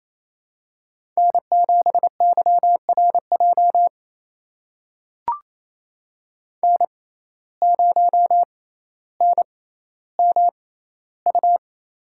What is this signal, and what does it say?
Morse code, heard as a keyed tone, about 28 wpm, spaced as for 14 wpm, 700 Hz. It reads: N7YRJ N0NMU
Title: USA Call Signs - 5 Characters - 6x Character Spacing - 28wpm